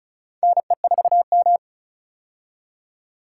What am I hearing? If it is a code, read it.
Morse code, heard as a keyed tone, about 36 wpm, 700 Hz. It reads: NE4M
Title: K1USN SST QSO Practice - 35wpm